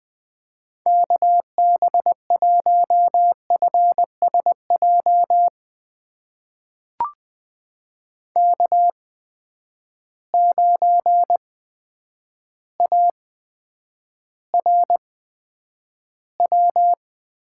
Morse code, heard as a keyed tone, about 20 wpm, spaced as for 9 wpm, 700 Hz. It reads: KB1FSJ K9ARW
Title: USA Call Signs - 8x Character Spacing - 20wpm